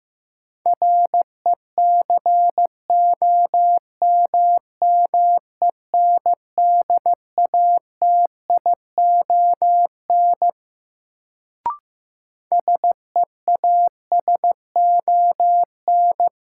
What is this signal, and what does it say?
Morse code, heard as a keyed tone, about 15 wpm, 700 Hz. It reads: RECOMMENDATION SEASON
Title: Sentences from Top 1400 Words - New Word Drill - 15wpm